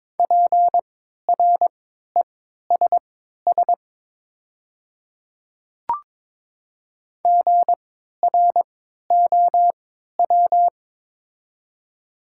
Morse code, heard as a keyed tone, about 22 wpm, spaced as for 12 wpm, 700 Hz. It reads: PRESS GROW
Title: 3x Character Spacing - Top 300 Words - 22wpm